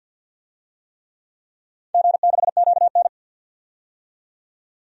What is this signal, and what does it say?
Morse code, heard as a keyed tone, about 49 wpm, 700 Hz. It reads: G6QN
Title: DXCC-100 Call Signs - 4 Characters - 50wpm